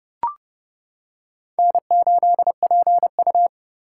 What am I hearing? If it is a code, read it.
Morse code, heard as a keyed tone, about 28 wpm, 700 Hz. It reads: N8PU